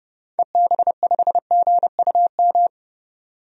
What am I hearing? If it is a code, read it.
Morse code, heard as a keyed tone, about 29 wpm, 700 Hz. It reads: EB5GUM
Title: DXCC-100 Call Signs - 6 Characters - 30wpm